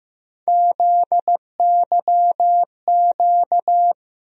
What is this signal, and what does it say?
Morse code, heard as a keyed tone, about 15 wpm, 700 Hz. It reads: ZYQ
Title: USA Call Sign Postfixes - 3 Letters 15wpm